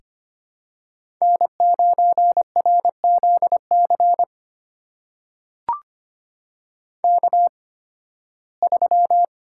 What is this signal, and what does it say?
Morse code, heard as a keyed tone, about 25 wpm, spaced as for 16 wpm, 700 Hz. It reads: N9RZC K3
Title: USA Call Signs - 5 Characters - 8x Character Spacing - 25wpm